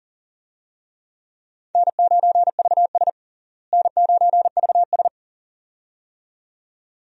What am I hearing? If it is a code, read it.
Morse code, heard as a keyed tone, about 40 wpm, 700 Hz. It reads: N9VS N9VS
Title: USA Call Signs - 4 Characters - Repeated 2x - 40wpm